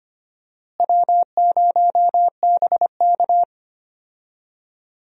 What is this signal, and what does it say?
Morse code, heard as a keyed tone, about 25 wpm, 700 Hz. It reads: W0BK
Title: USA Call Signs - 4 Characters - Repeated 2x - 25wpm